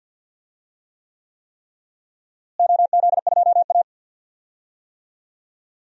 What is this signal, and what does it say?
Morse code, heard as a keyed tone, about 53 wpm, 700 Hz. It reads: OZ2A
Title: DXCC-100 Call Signs - 4 Characters - Repeated 3x - 50wpm